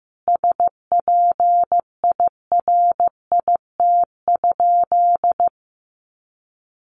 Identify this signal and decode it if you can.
Morse code, heard as a keyed tone, about 15 wpm, 700 Hz. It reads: SPIRIT?
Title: Sentences from Top 1400 Words 15wpm